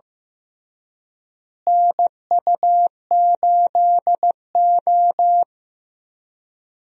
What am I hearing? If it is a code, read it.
Morse code, heard as a keyed tone, about 15 wpm, 700 Hz. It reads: NU8O
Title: USA Call Signs - 4 Characters - Repeated 3x - 15wpm